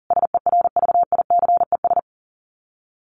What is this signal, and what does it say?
Morse code, heard as a keyed tone, about 37 wpm, 700 Hz. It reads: SERVICES
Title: Sets of 5 Words 40wpm